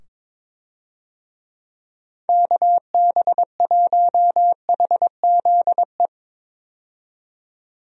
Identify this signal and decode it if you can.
Morse code, heard as a keyed tone, about 22 wpm, 700 Hz. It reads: KB1HZE
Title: USA Call Signs - 6 Characters - 22wpm